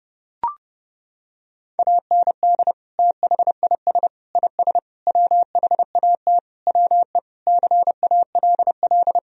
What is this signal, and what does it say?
Morse code, heard as a keyed tone, about 32 wpm, 700 Hz. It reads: AND THIS IS WHAT WE CALL
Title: Sentences from Top 1000 Words - Review Part 3 - 30wpm